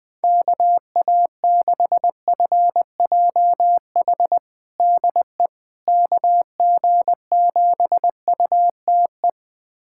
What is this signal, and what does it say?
Morse code, heard as a keyed tone, about 21 wpm, 700 Hz. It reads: KA6FJH DE KG7UTE
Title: USA Callsign DE Callsign - 6 Characters - 20wpm